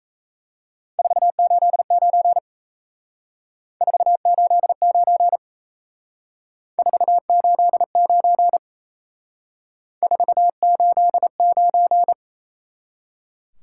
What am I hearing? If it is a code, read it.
Morse code, heard as a keyed tone, about 36 wpm, 700 Hz. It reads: 489 489 489 489